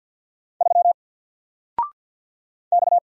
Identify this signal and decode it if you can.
Morse code, heard as a keyed tone, about 50 wpm, 700 Hz. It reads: 3 X
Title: Single Letter-Number 50wpm